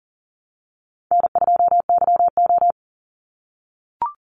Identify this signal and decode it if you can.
Morse code, heard as a keyed tone, about 40 wpm, 700 Hz. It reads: N2YO
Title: Easy Call Signs - 40wpm